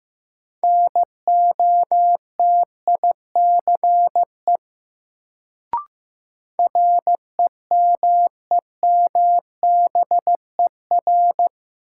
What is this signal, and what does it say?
Morse code, heard as a keyed tone, about 15 wpm, 700 Hz. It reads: NOTICE REMEMBER
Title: Sentences from Top 400 Words - New Word Drill - 15wpm